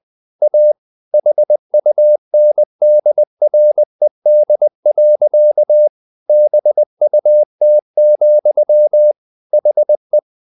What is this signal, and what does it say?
Morse code, heard as a keyed tone, about 20 wpm, 600 Hz. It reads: A HUNDRED. BUT, HE
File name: the old man and the sea / the old man and the sea part 1 20wpm